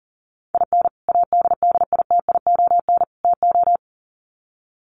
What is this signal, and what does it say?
Morse code, heard as a keyed tone, about 40 wpm, 700 Hz. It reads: IN ADDITION TO
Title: Sets of 3 Words 40wpm